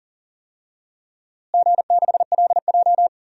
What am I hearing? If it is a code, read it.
Morse code, heard as a keyed tone, about 40 wpm, 700 Hz. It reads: G6LJ
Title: DXCC-100 Call Signs - 4 Characters - Repeated 3x - 40wpm